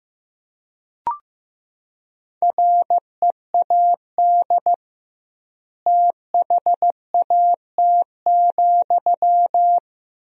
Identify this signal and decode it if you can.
Morse code, heard as a keyed tone, about 15 wpm, 700 Hz. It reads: READ THAT,
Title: 2x Word-Spacing - Sentences from Top 200 Words - 15wpm